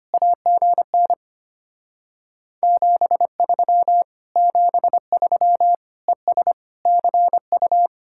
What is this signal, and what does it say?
Morse code, heard as a keyed tone, about 32 wpm, 700 Hz. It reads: AGN 73 73 ES CU